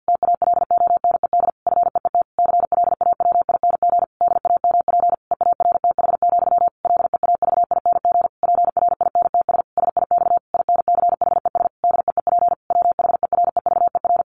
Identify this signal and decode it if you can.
Morse code, heard as a keyed tone, about 50 wpm, 700 Hz. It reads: TALONED FEET CLAWING DAMP EARTH, LEAVING PRINTS SIX INCHES DEEP WHEREVER
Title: A Sound Of Thunder 50wpm